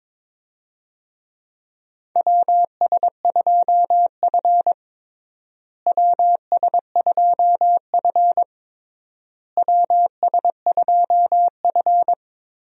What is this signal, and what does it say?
Morse code, heard as a keyed tone, about 22 wpm, 700 Hz. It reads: WS2F WS2F WS2F
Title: USA Call Signs - 4 Characters - Repeated 3x - 22wpm